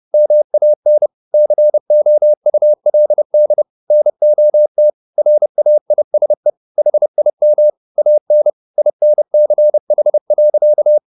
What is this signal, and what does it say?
Morse code, heard as a keyed tone, about 30 wpm, 600 Hz. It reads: MAN COULD NOT RAISE HIM AN INCH.
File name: the old man and the sea / the old man and the sea 1 30wpm